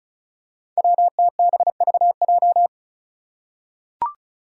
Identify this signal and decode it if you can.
Morse code, heard as a keyed tone, about 34 wpm, 700 Hz. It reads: WTBVJ
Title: Five Challenging Letter-Number Combinations 35wpm